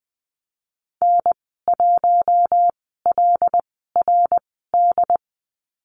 Morse code, heard as a keyed tone, about 19 wpm, spaced as for 14 wpm, 700 Hz. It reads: N1LRD
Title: 082 Lesson 14g - Number 2 - Call Signs 20wpm